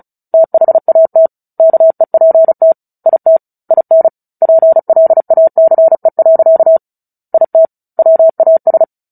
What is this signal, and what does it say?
Morse code, heard as a keyed tone, about 34 wpm, 650 Hz. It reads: THAT KEPT IT IN PLACE. IT WAS
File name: Book Two / 09-Chapter 9 - 35 WPM